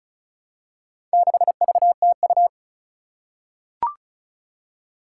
Morse code, heard as a keyed tone, about 36 wpm, 700 Hz. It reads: 6VTU
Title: Four Challenging Letter-Number Combinations 35wpm